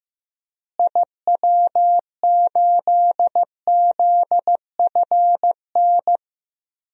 Morse code, heard as a keyed tone, about 16 wpm, 700 Hz. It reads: IW8ZFN
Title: DXCC-100 Call Signs - 6 Characters - Repeated 3x - 15wpm